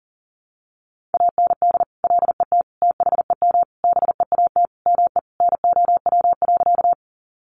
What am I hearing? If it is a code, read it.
Morse code, heard as a keyed tone, about 40 wpm, 700 Hz. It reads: AND LET THEM BEAT ME NOW.
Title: Sentences from Top 1300 Words 40wpm